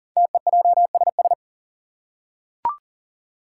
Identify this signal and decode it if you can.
Morse code, heard as a keyed tone, about 42 wpm, 700 Hz. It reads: TEJSS
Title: Code Groups - 5 Letters - 40wpm